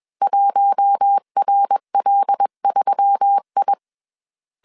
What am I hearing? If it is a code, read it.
Morse code, heard as a keyed tone, about 21 wpm, 800 Hz. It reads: 1RL3I
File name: CW / DET / 1 6 1500C S A 20wpm 2048